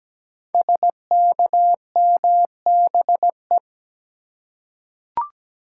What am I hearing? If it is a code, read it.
Morse code, heard as a keyed tone, about 17 wpm, 700 Hz. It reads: SKMBE
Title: Code Groups - 5 Letters - 17wpm